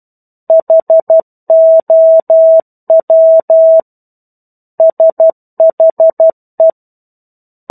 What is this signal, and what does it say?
Morse code, heard as a keyed tone, about 12 wpm, 650 Hz. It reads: HOW SHE